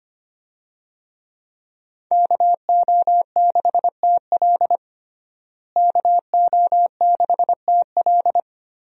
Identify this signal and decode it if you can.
Morse code, heard as a keyed tone, about 25 wpm, 700 Hz. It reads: KO6TL KO6TL
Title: USA Call Signs - 5 Characters - Repeated 2x - 25wpm